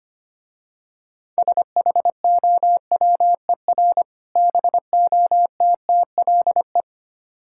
Morse code, heard as a keyed tone, about 26 wpm, 700 Hz. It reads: SHOWER BOTTLE
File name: Easy Words - Speed-Racing with Hints - 25wpm